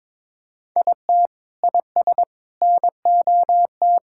Morse code, heard as a keyed tone, about 22 wpm, 700 Hz. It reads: IT IS NOT
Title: Sentences from Top 1000 Words - Review Part 6 - 22wpm